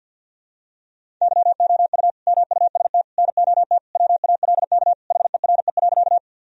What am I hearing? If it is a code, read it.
Morse code, heard as a keyed tone, about 53 wpm, 700 Hz. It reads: YOU MUST NOT WALK HERE.
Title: Sentences from Top 1000 Words - Review Part 1 - 50wpm